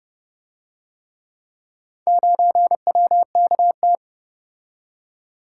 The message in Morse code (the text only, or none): 9WKT